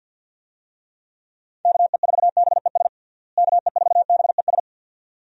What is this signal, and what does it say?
Morse code, heard as a keyed tone, about 51 wpm, 700 Hz. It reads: KE4BES KE4BES
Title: USA Call Signs - 6 Characters - Repeated 2x - 50wpm